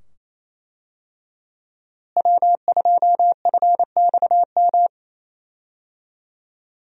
Morse code, heard as a keyed tone, about 28 wpm, 700 Hz. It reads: W2FXM
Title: USA Call Signs - 5 Characters - Repeated 2x - 28wpm